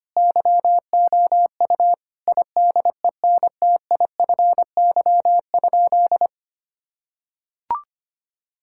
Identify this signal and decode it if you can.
Morse code, heard as a keyed tone, about 25 wpm, 700 Hz. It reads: YOU IDENTIFY?
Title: Sentences from Top 1000 Words - Review Part 6 - 25wpm